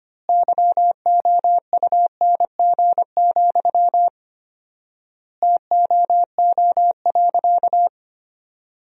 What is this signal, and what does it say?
Morse code, heard as a keyed tone, about 25 wpm, 700 Hz. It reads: YOUNG, TOO.
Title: 4x Word-Spacing - Sentences from Top 400 Words - 25wpm